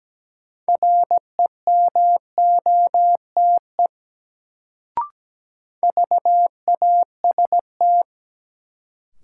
Morse code, heard as a keyed tone, about 17 wpm, 700 Hz. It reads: REMOTE VAST
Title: Sentences from Top 1300 Words - New Word Drill Alt - 17wpm